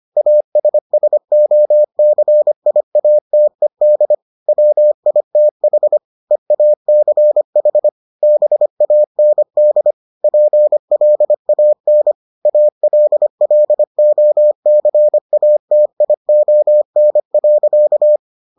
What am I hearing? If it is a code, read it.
Morse code, heard as a keyed tone, about 25 wpm, 600 Hz. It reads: ASSOCIATED WITH EACH BAND PLAN ALLOCATION.